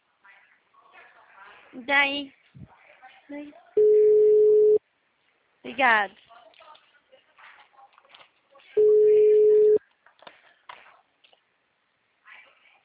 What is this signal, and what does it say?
Ringback tone (Europe, 425 Hz), with speech.